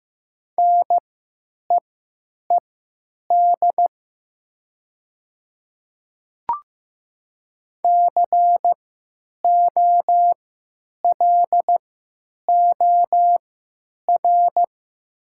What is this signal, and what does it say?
Morse code, heard as a keyed tone, about 15 wpm, spaced as for 9 wpm, 700 Hz. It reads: NEED COLOR
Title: 3x Character Spacing - Top 300 Words - 15wpm